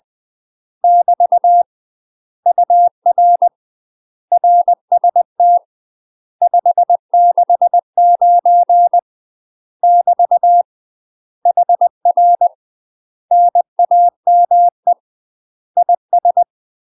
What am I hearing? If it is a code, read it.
Morse code, heard as a keyed tone, about 20 wpm, 700 Hz. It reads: <BT> UR RST 569 <BT> HR NAME IS